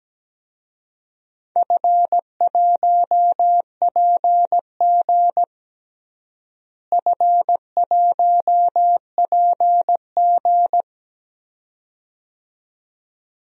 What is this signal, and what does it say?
Morse code, heard as a keyed tone, about 17 wpm, 700 Hz. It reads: F1PG F1PG
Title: DXCC-100 Call Signs - 4 Characters - Repeated 2x - 17wpm